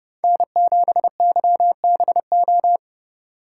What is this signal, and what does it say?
Morse code, heard as a keyed tone, about 30 wpm, 700 Hz. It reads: N7YBO
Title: USA Call Signs - 5 Characters - 2x Character Spacing - 30wpm